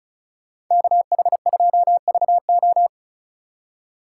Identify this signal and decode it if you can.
Morse code, heard as a keyed tone, about 36 wpm, 700 Hz. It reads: KH2VO
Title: USA Call Signs - 5 Characters - 3x Character Spacing - 35wpm